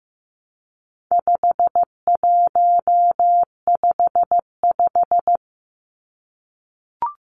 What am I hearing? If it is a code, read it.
Morse code, heard as a keyed tone, about 15 wpm, 700 Hz. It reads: 5155
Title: Numbers - 4-Digit 15wpm